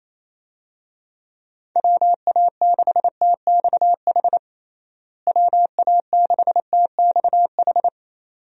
Morse code, heard as a keyed tone, about 28 wpm, 700 Hz. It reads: WA6TXH WA6TXH
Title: USA Call Signs - 6 Characters - Repeated 2x - 28wpm